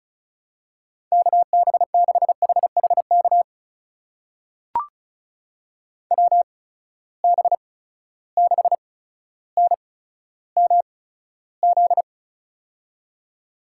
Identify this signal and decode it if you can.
Morse code, heard as a keyed tone, about 36 wpm, spaced as for 15 wpm, 700 Hz. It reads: KB6HHK WB6NMZ
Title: USA Call Signs - 6 Characters - 8x Character Spacing - 35wpm